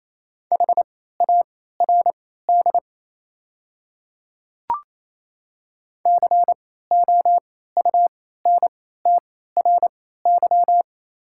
Morse code, heard as a keyed tone, about 28 wpm, spaced as for 16 wpm, 700 Hz. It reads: HARD COUNTRY
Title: 3x Character Spacing - Top 300 Words - 28wpm